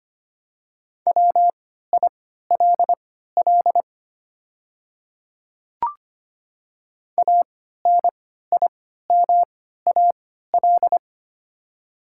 Morse code, heard as a keyed tone, about 25 wpm, spaced as for 14 wpm, 690 Hz. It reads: WILL ANIMAL
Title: 3x Character Spacing - Top 200 Words - 25wpm